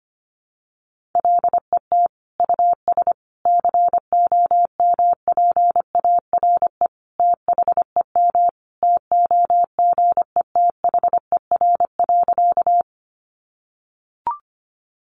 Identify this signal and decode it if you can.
Morse code, heard as a keyed tone, about 25 wpm, 700 Hz. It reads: LET US COMPARE THEM TOGETHER.
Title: Sentences from Top 1300 Words 25wpm